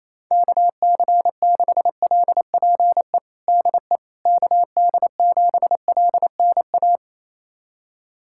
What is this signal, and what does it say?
Morse code, heard as a keyed tone, about 28 wpm, 700 Hz. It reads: KC6LPE DE KD7LNA
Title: USA Callsign DE Callsign - 6 Characters - 28wpm